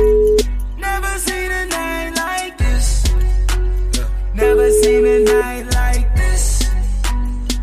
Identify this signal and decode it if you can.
Ringback tone (Europe, 425 Hz), with noise.